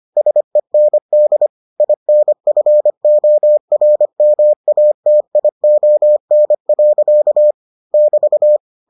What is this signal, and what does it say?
Morse code, heard as a keyed tone, about 25 wpm, 600 Hz. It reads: SEND INFORMATION. <BT>